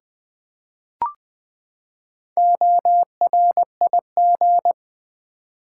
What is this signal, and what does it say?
Morse code, heard as a keyed tone, about 19 wpm, 700 Hz. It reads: ORIG
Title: Build A Word - 4 Letters At A Time - 10-Letter Words - 20wpm